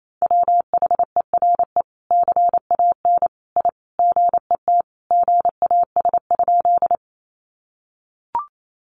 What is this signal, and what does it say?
Morse code, heard as a keyed tone, about 28 wpm, 700 Hz. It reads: WHERE CAN I GET GAS?
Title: Sentences from Top 1100 Words 28wpm